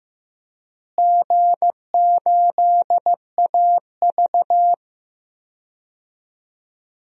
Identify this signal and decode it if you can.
Morse code, heard as a keyed tone, about 15 wpm, 700 Hz. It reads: G8AV
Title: DXCC-100 Call Signs - 4 Characters - Repeated 3x - 15wpm